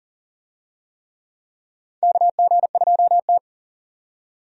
Morse code, heard as a keyed tone, about 40 wpm, 700 Hz. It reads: KG2T
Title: USA Call Signs - 4 Characters - 40wpm